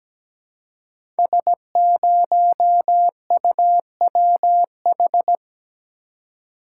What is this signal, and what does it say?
Morse code, heard as a keyed tone, about 17 wpm, 700 Hz. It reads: S0UWH
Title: Code Groups - 5 Letter:Numbers - 17wpm